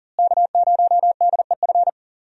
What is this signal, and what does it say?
Morse code, heard as a keyed tone, about 40 wpm, 700 Hz. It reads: K0DEF
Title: USA Call Signs - 5 Characters - Repeated 2x - 40wpm